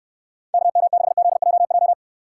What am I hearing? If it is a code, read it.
Morse code, heard as a keyed tone, about 65 wpm, 700 Hz. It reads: KM6ZJJ